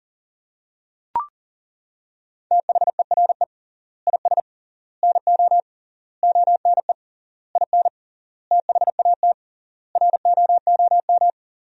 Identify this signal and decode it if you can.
Morse code, heard as a keyed tone, about 37 wpm, 700 Hz. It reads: THERE IS NO ONE IN THAT ROOM
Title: 3x Word-Spacing - Sentences from Top 500 Words - 40wpm